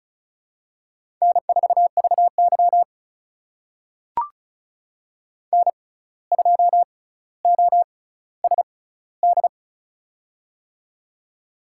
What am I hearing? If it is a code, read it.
Morse code, heard as a keyed tone, about 36 wpm, spaced as for 17 wpm, 700 Hz. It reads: N4VY N2OSD
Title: USA Call Signs - 6x Character Spacing - 35wpm